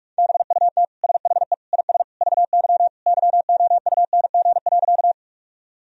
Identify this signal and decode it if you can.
Morse code, heard as a keyed tone, about 46 wpm, 700 Hz. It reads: BUT SHE IS VY YOUNG.
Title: Sentences from Top 400 Words and Abbreviations 45wpm